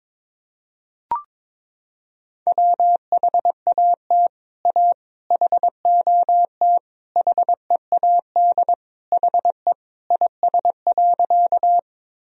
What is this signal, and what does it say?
Morse code, heard as a keyed tone, about 22 wpm, 700 Hz. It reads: WHAT A HOT HEAD HE IS.
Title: Sentences from Top 1000 Words - Review Part 3 - 22wpm